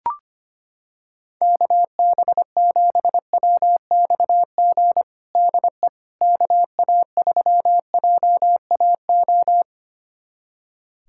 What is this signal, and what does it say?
Morse code, heard as a keyed tone, about 25 wpm, 700 Hz. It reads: KB7WXG DE KA3JAO
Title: USA Callsign DE Callsign 25wpm